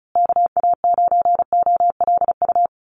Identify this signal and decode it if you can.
Morse code, heard as a keyed tone, about 34 wpm, 700 Hz. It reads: KA9OLU
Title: USA Call Signs Repeated 2x 35wpm